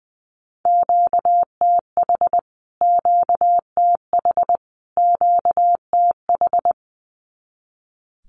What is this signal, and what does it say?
Morse code, heard as a keyed tone, about 20 wpm, 700 Hz. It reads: QTH QTH QTH
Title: Q-Codes Repeated 3x 20wpm